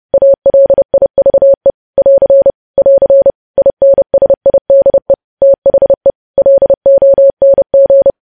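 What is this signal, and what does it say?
Morse code, heard as a keyed tone, about 30 wpm, 550 Hz. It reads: ALIVE <AR> <AR> INSIDE THE LONG
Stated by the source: News headlines sent at 30 wpm.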